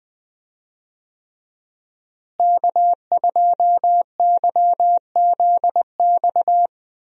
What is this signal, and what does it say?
Morse code, heard as a keyed tone, about 20 wpm, 700 Hz. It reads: K2YZX
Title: USA Call Signs - 5 Characters - Repeated 3x - 20wpm